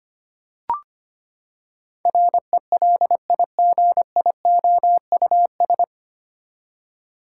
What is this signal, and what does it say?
Morse code, heard as a keyed tone, about 25 wpm, 700 Hz. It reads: RELIGIOUS
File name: Nine Letter Words - 25wpm